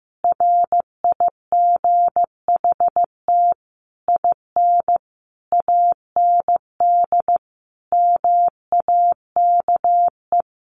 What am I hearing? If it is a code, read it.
Morse code, heard as a keyed tone, about 15 wpm, 700 Hz. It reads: RIGHT IN AND MAKE
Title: Sentences from Top 600 Words 15wpm